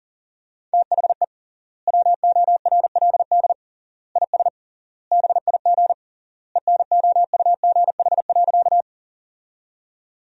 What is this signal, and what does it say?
Morse code, heard as a keyed tone, about 37 wpm, 700 Hz. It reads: THE WORLD IS BIG ENOUGH.
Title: 3x Word-Spacing - Sentences from Top 400 Words - 40wpm